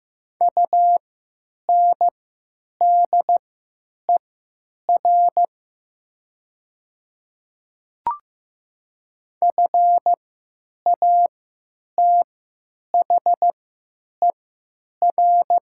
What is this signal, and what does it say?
Morse code, heard as a keyed tone, about 15 wpm, spaced as for 9 wpm, 700 Hz. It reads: UNDER FATHER
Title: 3x Character Spacing - Top 200 Words - 15wpm